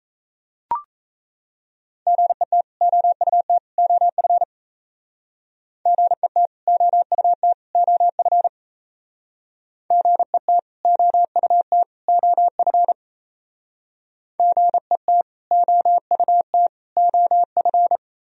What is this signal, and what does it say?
Morse code, heard as a keyed tone, about 36 wpm, 700 Hz. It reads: GET OUT OF GET OUT OF GET OUT OF GET OUT OF